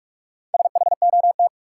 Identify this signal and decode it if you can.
Morse code, heard as a keyed tone, about 47 wpm, 700 Hz. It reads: SHOT
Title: Sentences from Top 1100 Words - New Word Drill - 45wpm